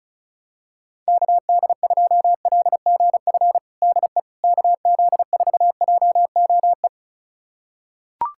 Morse code, heard as a keyed tone, about 36 wpm, 700 Hz. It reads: KD2LGF DE KZ4JOE
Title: USA Callsign DE Callsign - 6 Characters - 35wpm